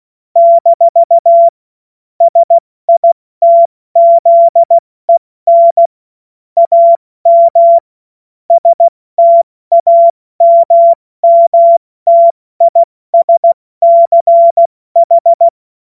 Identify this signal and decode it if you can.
Morse code, heard as a keyed tone, about 16 wpm, 680 Hz. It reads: - SITZEN AM STAMMTISCH
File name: Morsekurs / Witze / Witz 21.November 2023 70 F80